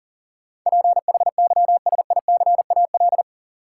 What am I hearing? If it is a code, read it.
Morse code, heard as a keyed tone, about 40 wpm, 700 Hz. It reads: PHYSICAL
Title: Sentences from Top 1400 Words - New Word Drill Alt - 40wpm